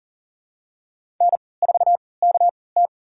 Morse code, heard as a keyed tone, about 40 wpm, spaced as for 23 wpm, 700 Hz. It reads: N4KT
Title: USA Call Signs - 4 Characters - 3x Character Spacing - 40wpm